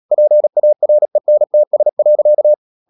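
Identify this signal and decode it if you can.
Morse code, heard as a keyed tone, about 37 wpm, 600 Hz. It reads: PARENTS.
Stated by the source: Morse practice at 36 wpm, text of Ken Follett's novel Never, from CW med Gnister.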